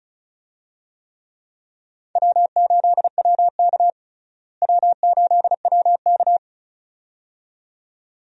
Morse code, heard as a keyed tone, about 34 wpm, 700 Hz. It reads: W8WK W8WK
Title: USA Call Signs - 4 Characters - Repeated 2x - 35wpm